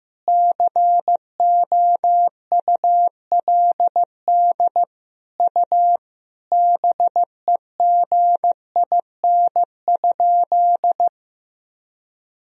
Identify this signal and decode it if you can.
Morse code, heard as a keyed tone, about 15 wpm, 700 Hz. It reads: COULD U BEGIN?
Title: Sentences from Top 300 Words and Abbreviations 15wpm